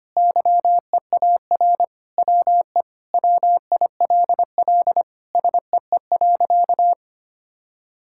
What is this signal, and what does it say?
Morse code, heard as a keyed tone, about 25 wpm, 700 Hz. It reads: YEAR WE WILL SEE.
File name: practice / 25wpm / Sentences with 3-Word Phrases - 25wpm